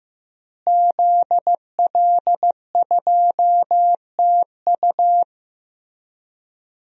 Morse code, heard as a keyed tone, about 16 wpm, 700 Hz. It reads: ZL2TU
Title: DXCC-100 Call Signs - 5 Characters - Repeated 3x - 15wpm